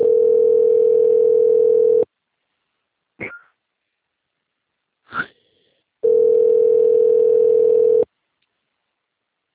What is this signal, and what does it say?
Ringback tone (North America), 440 + 480 Hz.